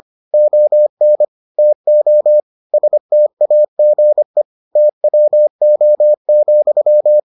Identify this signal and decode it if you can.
Morse code, heard as a keyed tone, about 25 wpm, 600 Hz. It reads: ON TO STAGE TWO,